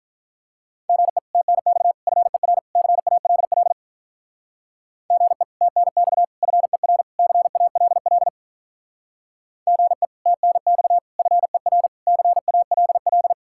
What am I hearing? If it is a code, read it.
Morse code, heard as a keyed tone, about 46 wpm, 700 Hz. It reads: GE TNX FER CALL GE TNX FER CALL GE TNX FER CALL